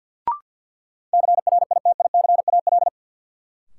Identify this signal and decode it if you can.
Morse code, heard as a keyed tone, about 53 wpm, 700 Hz. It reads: CRITICAL
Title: Sentences from Top 1100 Words - New Word Drill - 50wpm